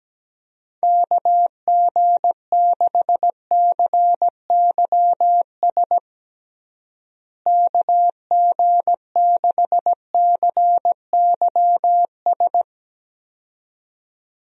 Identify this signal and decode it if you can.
Morse code, heard as a keyed tone, about 17 wpm, 700 Hz. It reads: KG6CYS KG6CYS
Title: USA Call Signs - 6 Characters - Repeated 2x - 17wpm